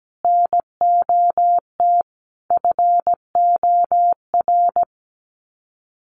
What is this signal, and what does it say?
Morse code, heard as a keyed tone, about 17 wpm, 700 Hz. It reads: NOT FOR
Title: Sets of 5 Words 17wpm